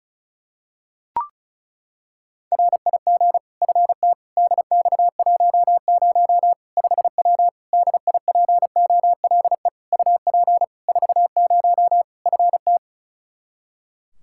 Morse code, heard as a keyed tone, about 34 wpm, 700 Hz. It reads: RIG FT DX10 5W DIPOLE UP 40 FT
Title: Practice QSO - Exchange 2 - Info - 35wpm